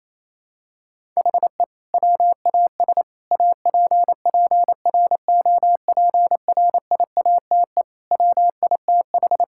Morse code, heard as a keyed tone, about 28 wpm, 700 Hz. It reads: HE WAS APPROPRIATE WITH
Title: Sentences from Top 1000 Words - Review Part 5 - 28wpm